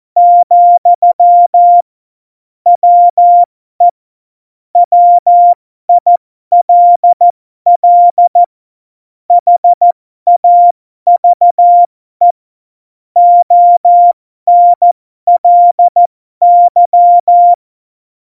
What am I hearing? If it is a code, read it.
Morse code, heard as a keyed tone, about 14 wpm, spaced as for 12 wpm, 700 Hz. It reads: , WE WILL HAVE ONLY
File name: Morsekurs / Witze